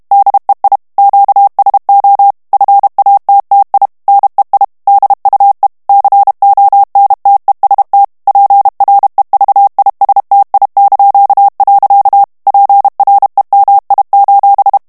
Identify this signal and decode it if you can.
Morse code, heard as a keyed tone, about 32 wpm, 800 Hz. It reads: DEI QSO FATTI NEI DUE CONTEST PREVISTI). PREMI:
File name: ITALIEN texte clair 2 vitesse 30 mots minute